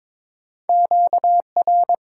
Morse code, heard as a keyed tone, about 22 wpm, 700 Hz. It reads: QR
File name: Top 90 QSO Elements - 6x Character Spacing - 22wpm